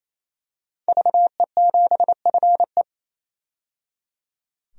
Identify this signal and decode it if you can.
Morse code, heard as a keyed tone, about 28 wpm, 700 Hz. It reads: VE7FE
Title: DXCC-100 Call Signs - 5 Characters - 28wpm